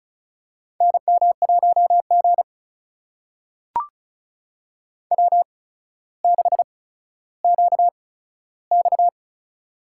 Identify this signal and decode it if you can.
Morse code, heard as a keyed tone, about 36 wpm, spaced as for 15 wpm, 700 Hz. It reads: NM1G W6QX
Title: USA Call Signs - 4 Characters - 8x Character Spacing - 35wpm